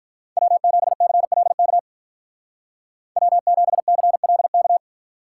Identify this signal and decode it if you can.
Morse code, heard as a keyed tone, about 43 wpm, 700 Hz. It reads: W7CLK W7CLK
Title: Speed Racing - USA 1x3 Call Signs 35wpm